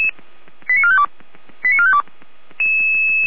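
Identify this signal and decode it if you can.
Five-tone selective call (ZVEI), heard as quick runs of five tones, tones of about 70 ms. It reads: 89452 89452
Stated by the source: ZVEI five-tone selective call 89452 received by SDR.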